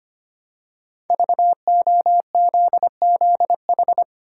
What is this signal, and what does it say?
Morse code, heard as a keyed tone, about 25 wpm, 700 Hz. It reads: VOZZH
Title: Code Groups - 5 Letter:Numbers - 25wpm